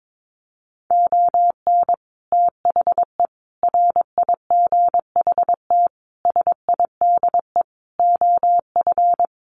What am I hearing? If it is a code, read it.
Morse code, heard as a keyed tone, about 22 wpm, 700 Hz. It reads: ON THE RIGHT SIDE OF
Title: Sets of 5 Words Encore 22wpm